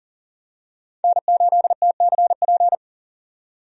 Morse code, heard as a keyed tone, about 40 wpm, 700 Hz. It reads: N8TCP
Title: USA Call Signs - 5 Characters - 40wpm